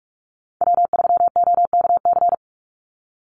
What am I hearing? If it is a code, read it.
Morse code, heard as a keyed tone, about 44 wpm, 700 Hz. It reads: W3OKC
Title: Call Signs Repeated 3x - 45wpm